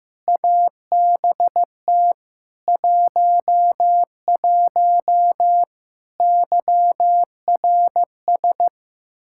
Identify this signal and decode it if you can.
Morse code, heard as a keyed tone, about 15 wpm, 700 Hz. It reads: ABT 11 YRS
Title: Practice QSO - Exchange 3 - More Info - 15wpm